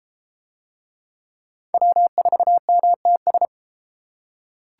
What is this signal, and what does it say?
Morse code, heard as a keyed tone, about 32 wpm, 700 Hz. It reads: W4MTS